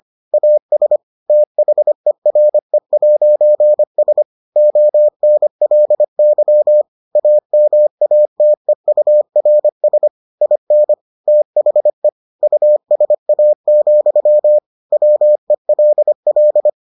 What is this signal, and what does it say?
Morse code, heard as a keyed tone, about 25 wpm, 600 Hz. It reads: AS THERE'S ONLY AMATEURS IN THE USA, WELL